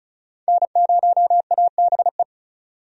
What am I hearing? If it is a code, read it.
Morse code, heard as a keyed tone, about 36 wpm, 700 Hz. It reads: N0ABE
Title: USA Call Signs - 5 Characters - 35wpm